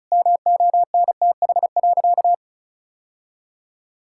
Morse code, heard as a keyed tone, about 34 wpm, 700 Hz. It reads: MONTH.
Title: Sentences from Top 1000 Words - Review Part 6 - 35wpm